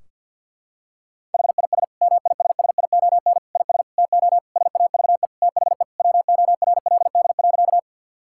Morse code, heard as a keyed tone, about 50 wpm, 700 Hz. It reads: HIS MISSION IS TO SAVE THE WORLD.
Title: Sentences from Top 1600 Words 50wpm